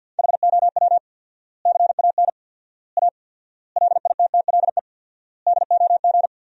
Morse code, heard as a keyed tone, about 52 wpm, 700 Hz. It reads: HOW CAN A LITTLE DOG
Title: 4x Word-Spacing - Sentences from Top 500 Words - 50wpm